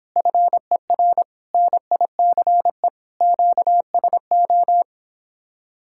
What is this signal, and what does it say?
Morse code, heard as a keyed tone, about 26 wpm, 700 Hz. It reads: FER NICE QSO